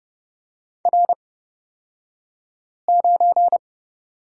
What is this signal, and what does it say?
Morse code, heard as a keyed tone, about 30 wpm, spaced as for 5 wpm, 700 Hz. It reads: R9
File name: Lesson 37d - Letter x - Two Characters 30wpm